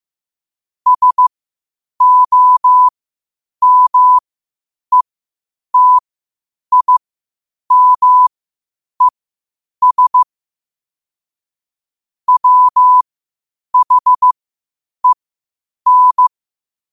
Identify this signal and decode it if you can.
Morse code, heard as a keyed tone, about 15 wpm, spaced as for 8 wpm, 1000 Hz. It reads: SOMETIMES WHEN